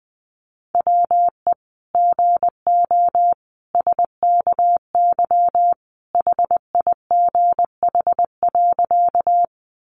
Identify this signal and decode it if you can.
Morse code, heard as a keyed tone, about 20 wpm, 700 Hz. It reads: WE GO SKY HIGH.
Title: Sentences from Top 1200 Words 20wpm